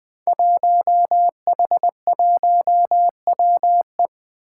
Morse code, heard as a keyed tone, about 20 wpm, 700 Hz. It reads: 1H1WE
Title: Five Challenging Letter-Number Combinations 20wpm